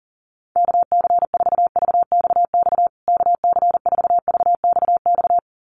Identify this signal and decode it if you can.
Morse code, heard as a keyed tone, about 40 wpm, 700 Hz. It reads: KC4VXX KC4VXX
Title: USA Call Signs Repeated 2x 40wpm